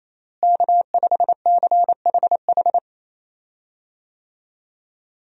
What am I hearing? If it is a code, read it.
Morse code, heard as a keyed tone, about 28 wpm, 700 Hz. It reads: K5CHH
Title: USA Call Signs - 5 Characters - Repeated 3x - 28wpm